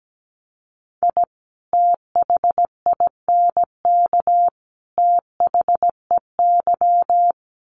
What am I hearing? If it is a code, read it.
Morse code, heard as a keyed tone, about 17 wpm, 700 Hz. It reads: I THINK THEY
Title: Sets of 3 Words 17wpm